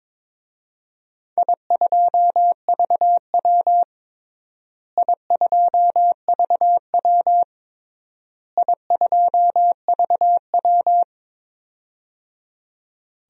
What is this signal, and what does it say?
Morse code, heard as a keyed tone, about 22 wpm, 700 Hz. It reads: I2VW I2VW I2VW
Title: DXCC-100 Call Signs - 4 Characters - Repeated 3x - 22wpm